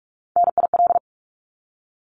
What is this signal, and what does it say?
Morse code, heard as a keyed tone, about 44 wpm, 700 Hz. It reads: NIL
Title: Abbreviations 45wpm